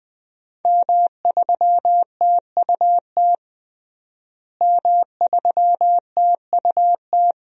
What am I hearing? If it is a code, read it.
Morse code, heard as a keyed tone, about 20 wpm, 700 Hz. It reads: M3TUT M3TUT
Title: DXCC-100 Call Signs - 5 Characters - Repeated 2x - 20wpm